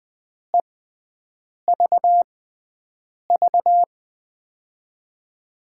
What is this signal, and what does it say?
Morse code, heard as a keyed tone, about 20 wpm, spaced as for 7 wpm, 700 Hz. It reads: EVV
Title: USA Call Signs - 5 Characters - 6x Character Spacing - 20wpm